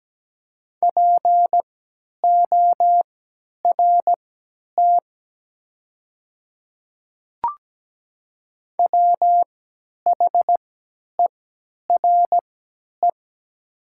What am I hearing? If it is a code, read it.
Morse code, heard as a keyed tone, about 17 wpm, spaced as for 10 wpm, 700 Hz. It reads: PORT WHERE
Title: 3x Character Spacing - Top 200 Words - 17wpm